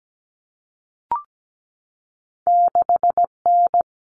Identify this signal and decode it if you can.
Morse code, heard as a keyed tone, about 17 wpm, 700 Hz. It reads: 6N
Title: All Cut Numbers - 2-Digit 17wpm